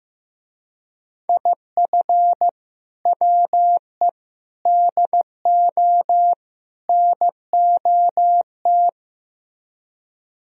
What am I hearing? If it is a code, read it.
Morse code, heard as a keyed tone, about 16 wpm, 700 Hz. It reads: IF WE DO NOT
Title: Speed-Racing - Sets of 4 Words 15wpm